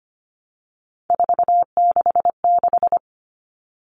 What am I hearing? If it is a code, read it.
Morse code, heard as a keyed tone, about 25 wpm, 700 Hz. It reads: 466
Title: Numbers - 3-Digit 25wpm